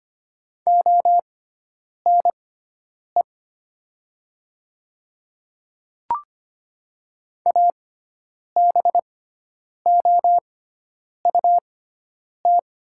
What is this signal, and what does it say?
Morse code, heard as a keyed tone, about 25 wpm, spaced as for 9 wpm, 700 Hz. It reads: ONE ABOUT
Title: 6x Character Spacing - Top 500 Words - 25wpm